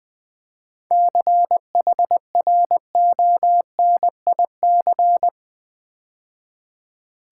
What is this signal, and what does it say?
Morse code, heard as a keyed tone, about 20 wpm, 690 Hz. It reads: CHRONIC
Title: 7 Letter Words - First 3 Letters Given - 20wpm